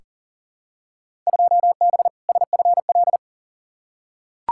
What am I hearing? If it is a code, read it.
Morse code, heard as a keyed tone, about 40 wpm, 700 Hz. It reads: 2B SFL
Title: Field Day Exchanges Encore - 40wpm